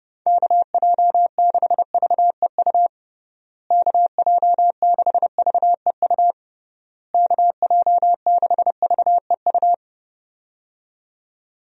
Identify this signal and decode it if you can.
Morse code, heard as a keyed tone, about 32 wpm, 700 Hz. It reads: KJ6VEU KJ6VEU KJ6VEU
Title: USA Call Signs - 6 Characters - Repeated 3x - 30wpm